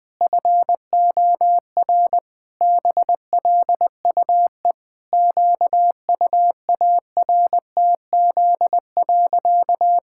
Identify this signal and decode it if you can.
Morse code, heard as a keyed tone, about 20 wpm, 700 Hz. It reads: FOR BLUE QUARTZ.
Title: Pangram Challenge v2 - 20wpm